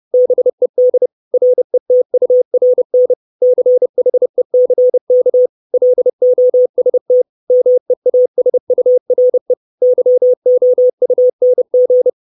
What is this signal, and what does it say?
Morse code, heard as a keyed tone, about 30 wpm, 500 Hz. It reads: BED RETURN CHECK LOST MEASURE YOUNG
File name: 632 Words 30WPM